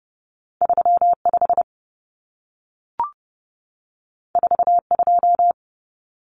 Morse code, heard as a keyed tone, about 30 wpm, 700 Hz. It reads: 35 42
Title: Common Cut Numbers - 2-Digit 30wpm